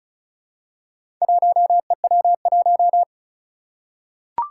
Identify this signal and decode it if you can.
Morse code, heard as a keyed tone, about 36 wpm, 700 Hz. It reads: 1EW1
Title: Four Challenging Letter-Number Combinations 35wpm